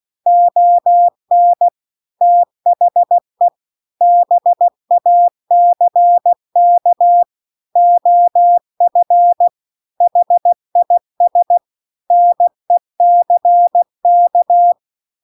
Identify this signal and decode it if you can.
Morse code, heard as a keyed tone, about 16 wpm, 700 Hz. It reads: ON THE BACK OF HIS NECK
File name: CW-The-Big-Twoheaded-River-2-16WPM